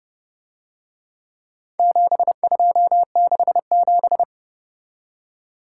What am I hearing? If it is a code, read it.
Morse code, heard as a keyed tone, about 30 wpm, 700 Hz. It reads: 7267